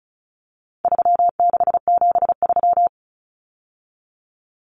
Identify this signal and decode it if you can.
Morse code, heard as a keyed tone, about 34 wpm, 700 Hz. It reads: 3673
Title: Common Cut Numbers - 4-Digit 35wpm